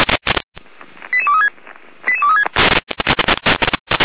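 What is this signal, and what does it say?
Five-tone selective call (ZVEI), heard as quick runs of five tones, tones of about 70 ms. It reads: 88236 88236
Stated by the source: ZVEI five-tone selective call 88236 received on an SDR.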